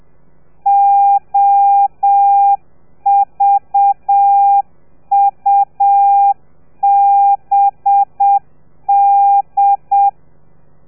Morse code, heard as a keyed tone, about 7 wpm, 800 Hz. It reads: OVUBD
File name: telegrafia / i2tfj / Lez4 30